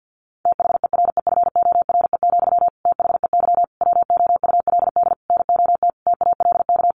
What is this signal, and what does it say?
Morse code, heard as a keyed tone, about 50 wpm, 700 Hz. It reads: THEREFORE, THEY WOULD NOT TALK
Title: Sentences from Top 800 Words 50wpm